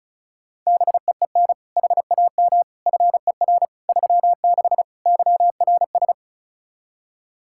Morse code, heard as a keyed tone, about 34 wpm, 700 Hz. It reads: BEEN HAM FER 36 YRS
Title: Practice QSO - Exchange 3 - More Info - 35wpm